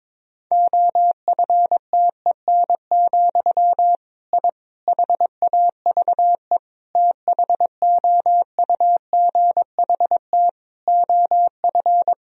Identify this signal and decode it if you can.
Morse code, heard as a keyed tone, about 22 wpm, 700 Hz. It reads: OFTEN, I HAVE THOUGHT OF
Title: Sentences from Top 1000 Words - Review Part 1 - 22wpm